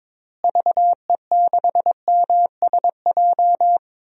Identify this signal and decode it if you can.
Morse code, heard as a keyed tone, about 22 wpm, 700 Hz. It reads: VE6MSJ
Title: DXCC-100 Call Signs - 6 Characters - Repeated 3x - 22wpm